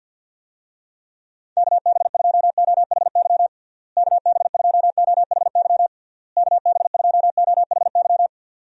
Morse code, heard as a keyed tone, about 52 wpm, 700 Hz. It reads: KB2OHY KB2OHY KB2OHY
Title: USA Call Signs - 6 Characters - Repeated 3x - 50wpm